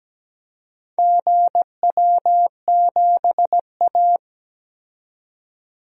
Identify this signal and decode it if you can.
Morse code, heard as a keyed tone, about 17 wpm, 700 Hz. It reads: GW7A
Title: DXCC-100 Call Signs - 4 Characters - 17wpm